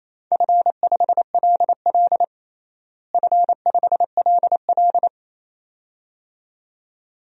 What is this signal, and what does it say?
Morse code, heard as a keyed tone, about 28 wpm, 700 Hz. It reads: F5LL F5LL
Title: DXCC-100 Call Signs - 4 Characters - Repeated 2x - 28wpm